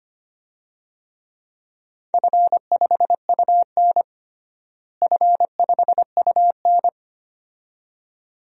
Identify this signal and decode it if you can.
Morse code, heard as a keyed tone, about 25 wpm, 700 Hz. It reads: F5UN F5UN
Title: DXCC-100 Call Signs - 4 Characters - Repeated 2x - 25wpm